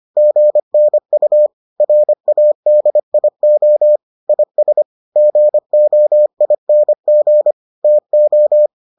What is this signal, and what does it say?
Morse code, heard as a keyed tone, about 25 wpm, 600 Hz. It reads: GNU RADIO IS GOING TO